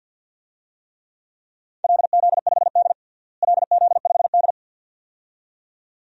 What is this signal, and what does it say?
Morse code, heard as a keyed tone, about 51 wpm, 700 Hz. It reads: LZ5D LZ5D
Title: DXCC-100 Call Signs - 4 Characters - Repeated 2x - 50wpm